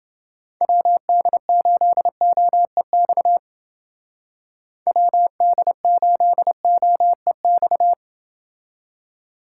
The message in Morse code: WD8OEX WD8OEX

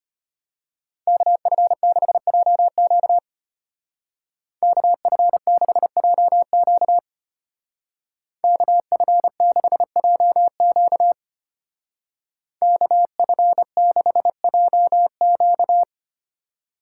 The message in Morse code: KF6JQ KF6JQ KF6JQ KF6JQ

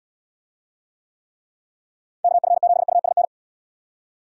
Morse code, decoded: KF7USN

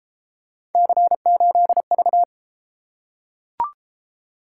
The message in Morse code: C8V

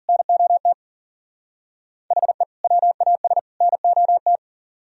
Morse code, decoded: NOT HE WAS NOT

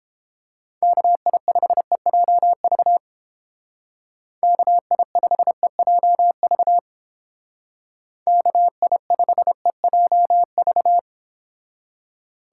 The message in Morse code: KI5EJV KI5EJV KI5EJV

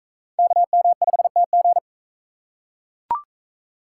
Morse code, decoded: KM5TG

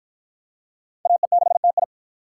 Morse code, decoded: AE6TI

